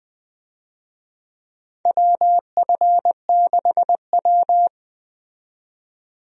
Morse code, heard as a keyed tone, about 20 wpm, 700 Hz. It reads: WF6W
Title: USA Call Signs - 4 Characters - 20wpm